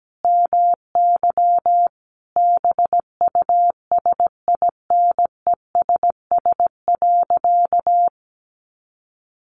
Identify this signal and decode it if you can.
Morse code, heard as a keyed tone, about 17 wpm, 700 Hz. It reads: MY BUSINESS.
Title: Sentences from Top 600 Words 17wpm